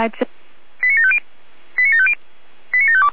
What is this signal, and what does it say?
Five-tone selective call (ZVEI), heard as quick runs of five tones, tones of about 70 ms, with speech. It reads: 89750 89750 89752